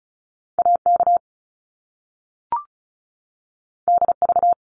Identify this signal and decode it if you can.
Morse code, heard as a keyed tone, about 36 wpm, 700 Hz. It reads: AK DV
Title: Top Two Letter Combinations in Words 35wpm